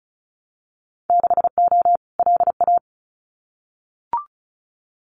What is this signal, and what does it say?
Morse code, heard as a keyed tone, about 34 wpm, 700 Hz. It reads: 6O LA
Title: Winter Field Day Exchanges Encore 35wpm